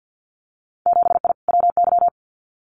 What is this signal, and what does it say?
Morse code, heard as a keyed tone, about 50 wpm, 700 Hz. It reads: 7I WY
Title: Winter Field Day Exchanges Encore 50wpm